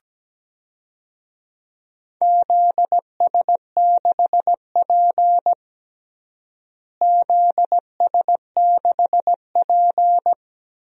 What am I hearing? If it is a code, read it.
Morse code, heard as a keyed tone, about 17 wpm, 700 Hz. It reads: ZS6P ZS6P
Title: DXCC-100 Call Signs - 4 Characters - Repeated 3x - 17wpm